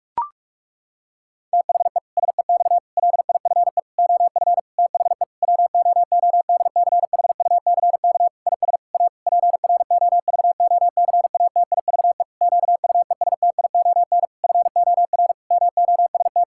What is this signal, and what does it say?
Morse code, heard as a keyed tone, about 44 wpm, 700 Hz. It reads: THE SEX LIFE OF THE WOODCHUCK IS A PROVOCATIVE QUESTION FOR MOST
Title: Pangram Challenge v2 - 45wpm